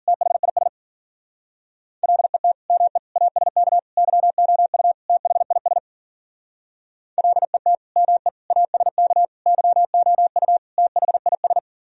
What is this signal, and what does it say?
Morse code, heard as a keyed tone, about 51 wpm, 700 Hz. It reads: THIS LET ME ASK YOU THIS LET ME ASK YOU THIS